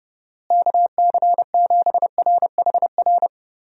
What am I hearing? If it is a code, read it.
Morse code, heard as a keyed tone, about 32 wpm, 700 Hz. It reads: KC7RHR
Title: USA Call Signs - 6 Characters - Repeated 3x - 30wpm